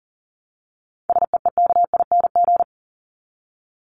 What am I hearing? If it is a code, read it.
Morse code, heard as a keyed tone, about 40 wpm, 700 Hz. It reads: SEEKING
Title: ING Suffix - Top 100 Words with Base Words 40wpm